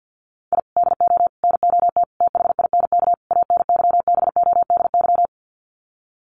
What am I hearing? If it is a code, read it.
Morse code, heard as a keyed tone, about 52 wpm, 700 Hz. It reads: I DO NOT THINK ANYBODY
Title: Sets of 5 Words 50wpm